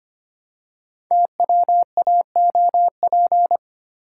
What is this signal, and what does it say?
Morse code, heard as a keyed tone, about 25 wpm, 700 Hz. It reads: TWAOP
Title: Code Groups - 5 Letters - 25wpm